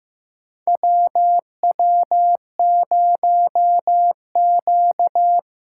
Morse code, heard as a keyed tone, about 15 wpm, 700 Hz. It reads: WW0Q
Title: USA Call Signs - 8x Character Spacing - 15wpm